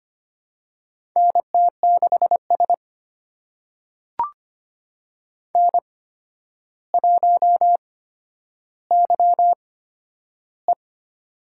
Morse code, heard as a keyed tone, about 25 wpm, spaced as for 11 wpm, 700 Hz. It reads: NT6S N1YE
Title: USA Call Signs - 4 Characters - 8x Character Spacing - 25wpm